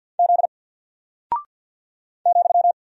Morse code, heard as a keyed tone, about 49 wpm, 700 Hz. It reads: Z ,
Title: Single Character (Letter:Number:Punctuation:Prosign) Alt 50wpm